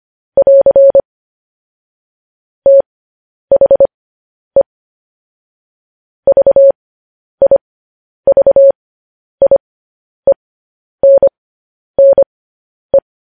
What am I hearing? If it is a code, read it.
Morse code, heard as a keyed tone, about 25 wpm, spaced as for 10 wpm, 550 Hz. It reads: <AR> THE VIVIENNE
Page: News Headlines in Morse Code at 10 WPM